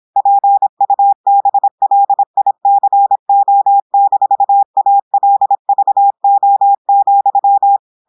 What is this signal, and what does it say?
Morse code, heard as a keyed tone, about 26 wpm, 800 Hz. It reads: PUBLICO-ALVO,
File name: portugais texte en clair 2 vitesse 26 mots minutes